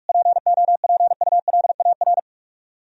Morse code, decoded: POPULAR

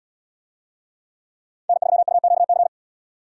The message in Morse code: N3UQC

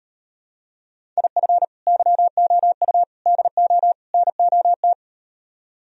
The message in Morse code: IF YOU DO NOT

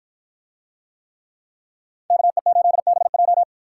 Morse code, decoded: KE8BJ

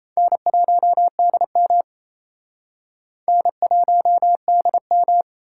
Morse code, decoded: N1DM N1DM

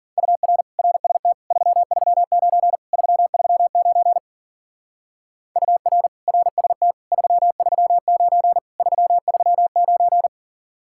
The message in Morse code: UR RST 339 339 UR RST 339 339

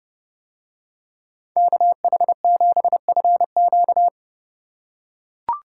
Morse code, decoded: KH7FQ

